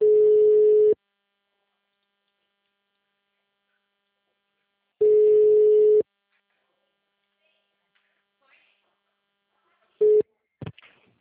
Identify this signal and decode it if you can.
Ringback tone (Europe, 425 Hz).
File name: out- -2009- - - .33910